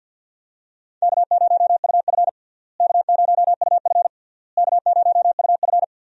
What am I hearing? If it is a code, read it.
Morse code, heard as a keyed tone, about 47 wpm, 700 Hz. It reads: K0UF K0UF K0UF
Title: USA Call Signs - 4 Characters - Repeated 3x - 50wpm